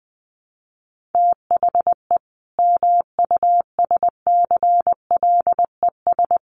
Morse code, heard as a keyed tone, about 20 wpm, 700 Hz. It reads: THE MUSCLES
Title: A Sound Of Thunder 20wpm